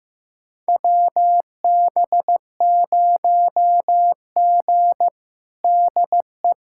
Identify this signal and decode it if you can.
Morse code, heard as a keyed tone, about 15 wpm, 700 Hz. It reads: WB0G DE
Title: USA Callsign DE Callsign - 4 Characters - 15wpm